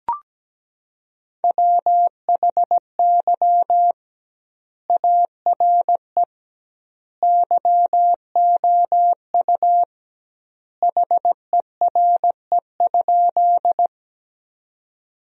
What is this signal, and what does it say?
Morse code, heard as a keyed tone, about 17 wpm, 700 Hz. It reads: WHY ARE YOU HERE?
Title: 2x Word-Spacing - Sentences from Top 200 Words - 17wpm